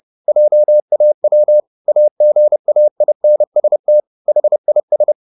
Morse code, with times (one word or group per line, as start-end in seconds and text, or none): 0.3-1.6 JAW
1.9-4.0 AGAINST
4.3-5.1 HIS